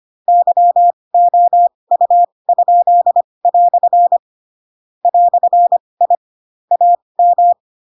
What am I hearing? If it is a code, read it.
Morse code, heard as a keyed tone, about 25 wpm, spaced as for 20 wpm, 700 Hz. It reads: YOU?" "I AM
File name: Pinocchio / Pinocchio ch4 15 25 ik1hge